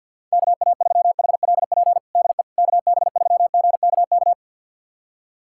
Morse code, heard as a keyed tone, about 51 wpm, 700 Hz. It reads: KA3HLP DE KB3GKK
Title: USA Callsign DE Callsign 50wpm